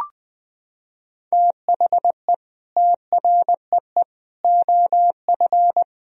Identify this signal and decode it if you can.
Morse code, heard as a keyed tone, about 20 wpm, 700 Hz. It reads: THE TREE OF
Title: Sentences from Top 1000 Words - Review Part 3 - 20wpm